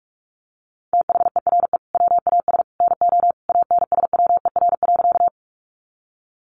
Morse code, heard as a keyed tone, about 44 wpm, 700 Hz. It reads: THERE WAS NO ANSWER.
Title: Sentences from Top 300 Words 45wpm